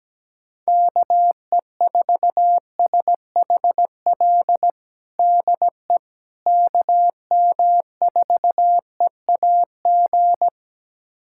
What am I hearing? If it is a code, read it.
Morse code, heard as a keyed tone, about 17 wpm, 700 Hz. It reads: KE4SHL DE KM4EAG
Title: USA Callsign DE Callsign - 6 Characters - 17wpm